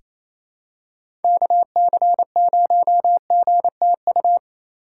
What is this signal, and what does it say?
Morse code, heard as a keyed tone, about 28 wpm, 700 Hz. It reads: KC0GTU
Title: USA Call Signs - 6x Character Spacing - 28wpm